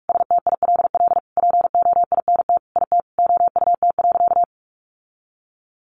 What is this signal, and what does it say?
Morse code, heard as a keyed tone, about 44 wpm, 700 Hz. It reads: STILL POINT IT OUT.
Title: Sentences from Top 1500 Words 45wpm